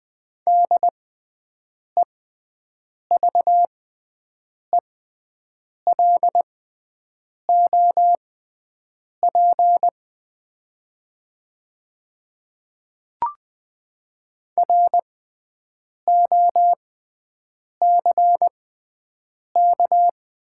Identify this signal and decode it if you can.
Morse code, heard as a keyed tone, about 20 wpm, spaced as for 7 wpm, 700 Hz. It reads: DEVELOP ROCK
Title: 6x Character Spacing - Top 500 Words - 20wpm